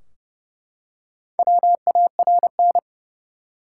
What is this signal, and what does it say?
Morse code, heard as a keyed tone, about 31 wpm, 700 Hz. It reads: WARN
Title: Four Letter Words - 30wpm